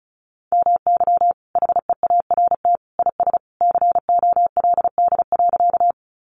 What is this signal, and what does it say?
Morse code, heard as a keyed tone, about 34 wpm, 700 Hz. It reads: MY HEART IS COLD.
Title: Sentences from Top 1200 Words 35wpm